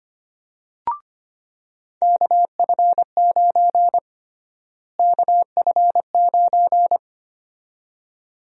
Morse code, heard as a keyed tone, about 25 wpm, 700 Hz. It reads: KF9 KF9
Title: USA Call Sign Prefixes - Repeated 2x - 25wpm